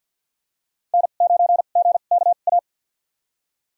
Morse code, heard as a keyed tone, about 47 wpm, spaced as for 35 wpm, 700 Hz. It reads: N9GKA
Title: USA Call Signs - 5 Characters - 2x Character Spacing - 50wpm